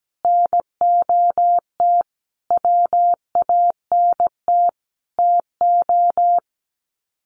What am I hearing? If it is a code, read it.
Morse code, heard as a keyed tone, about 17 wpm, 700 Hz. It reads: NOT WANT TO
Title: Sets of 5 Words 17wpm